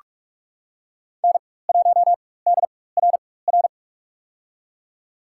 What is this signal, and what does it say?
Morse code, heard as a keyed tone, about 48 wpm, spaced as for 21 wpm, 700 Hz. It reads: N1DRR
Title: USA Call Signs - 5 Characters - 4x Character Spacing - 45wpm